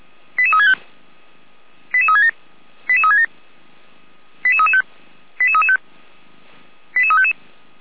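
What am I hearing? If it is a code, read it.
Five-tone selective call (ZVEI), heard as quick runs of five tones, tones of about 70 ms. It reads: 88365 88367 88367 88335 88335 88355
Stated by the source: ZVEI five-tone selective call received with an SDR.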